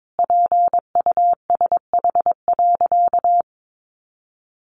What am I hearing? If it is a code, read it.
Morse code, heard as a keyed tone, about 22 wpm, 700 Hz. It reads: PUSH.
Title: Sentences from Top 1300 Words 22wpm